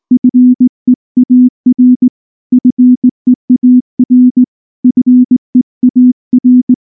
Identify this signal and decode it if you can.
Morse code, heard as a keyed tone, about 18 wpm, 250 Hz. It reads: FEAR FEAR FEAR
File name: morse-training-20WPM